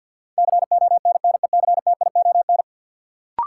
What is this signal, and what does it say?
Morse code, heard as a keyed tone, about 52 wpm, 700 Hz. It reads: CONNECTION
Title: Sentences from Top 1300 Words - New Word Drill - 50wpm